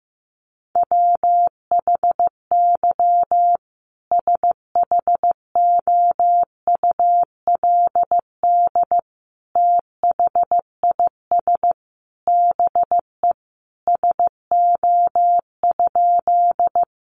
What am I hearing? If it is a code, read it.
Morse code, heard as a keyed tone, about 15 wpm, 700 Hz. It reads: WHY SHOULD THIS BE SO?
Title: Sentences from Top 300 Words 15wpm